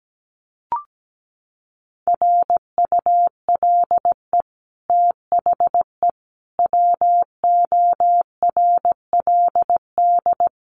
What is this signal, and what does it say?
Morse code, heard as a keyed tone, about 17 wpm, 700 Hz. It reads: RULE THE WORLD
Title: Sentences from Top 600 Words 17wpm